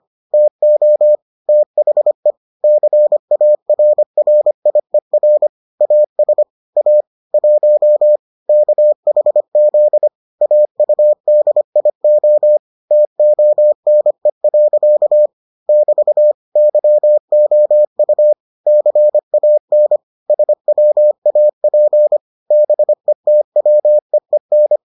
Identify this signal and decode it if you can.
Morse code, heard as a keyed tone, about 25 wpm, 600 Hz. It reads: TO THE CARRIER AS A 1 KHZ AUDIO TONE. <BT> YOU CAN SWAP BETWEEN